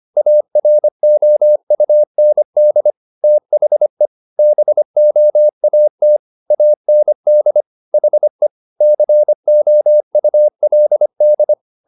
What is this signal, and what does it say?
Morse code, heard as a keyed tone, about 25 wpm, 600 Hz. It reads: AROUND THE BOAT AND HE COULD
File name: the old man and the sea part 1 25wpm